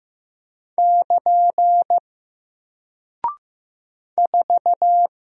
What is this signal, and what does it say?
Morse code, heard as a keyed tone, about 15 wpm, 700 Hz. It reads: <KN> 4
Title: Single Character (Letter:Number:Punctuation:Prosign) Alt 15wpm